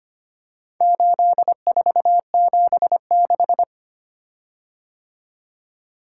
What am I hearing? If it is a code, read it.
Morse code, heard as a keyed tone, about 25 wpm, 700 Hz. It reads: 8476
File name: practice / 25wpm / Four Letter-Number Combinations - 25wpm